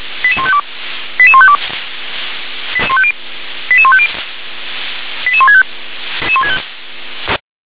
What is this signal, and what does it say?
Five-tone selective call (ZVEI), heard as quick runs of five tones, tones of about 70 ms. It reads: 88152 88152 88150 88165
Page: /357/SDR/ZVEI/